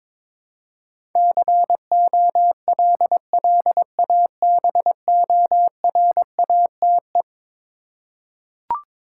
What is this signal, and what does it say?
Morse code, heard as a keyed tone, about 22 wpm, 700 Hz. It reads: COLLABORATE
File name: practice / 22wpm / Eleven Letter Words - 22wpm